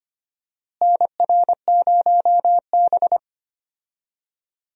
Morse code, heard as a keyed tone, about 25 wpm, 700 Hz. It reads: NR0B
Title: USA Call Signs - 4 Characters - Repeated 3x - 25wpm